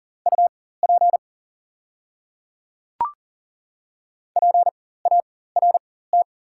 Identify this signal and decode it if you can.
Morse code, heard as a keyed tone, about 39 wpm, spaced as for 19 wpm, 690 Hz. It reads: UP PART
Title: 4x Character Spacing - Top 200 Words - 40wpm